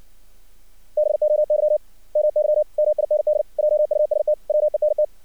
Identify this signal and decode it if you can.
Morse code, heard as a keyed tone, about 55 wpm, 600 Hz. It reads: BOY MY MIND JUST WENT
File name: archived / CRSnet